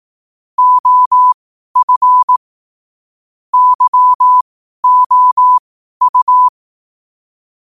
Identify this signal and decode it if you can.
Morse code, heard as a keyed tone, about 18 wpm, spaced as for 12 wpm, 1000 Hz. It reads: OF YOU
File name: qotdfiles / 12wpm